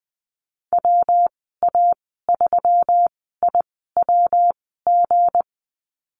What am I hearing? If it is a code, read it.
Morse code, heard as a keyed tone, about 20 wpm, spaced as for 14 wpm, 700 Hz. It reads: WA3IWG
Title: 179 Lesson 29g - Slash - Call Signs 20wpm